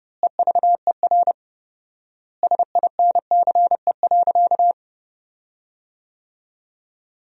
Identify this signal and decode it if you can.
Morse code, heard as a keyed tone, about 32 wpm, 700 Hz. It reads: EVER SINCE.
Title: 4x Word-Spacing - Sentences from Top 400 Words - 30wpm